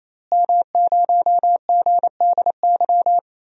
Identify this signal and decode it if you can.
Morse code, heard as a keyed tone, about 28 wpm, 700 Hz. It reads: M0GDY
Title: DXCC-100 Call Signs - 5 Characters - Repeated 3x - 28wpm